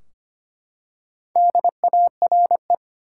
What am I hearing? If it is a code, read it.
Morse code, heard as a keyed tone, about 25 wpm, 700 Hz. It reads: DARE
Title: Four Letter Words - 25wpm